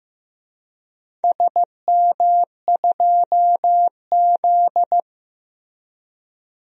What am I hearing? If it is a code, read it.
Morse code, heard as a keyed tone, about 15 wpm, 700 Hz. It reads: SM2Z
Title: DXCC-100 Call Signs - 4 Characters - Repeated 3x - 15wpm